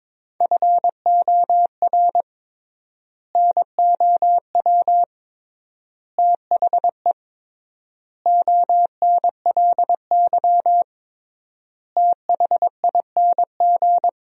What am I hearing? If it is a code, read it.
Morse code, heard as a keyed tone, about 22 wpm, 700 Hz. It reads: FOR NOW THE ONLY THING
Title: 3x Word-Spacing - Sentences from Top 200 Words - 22wpm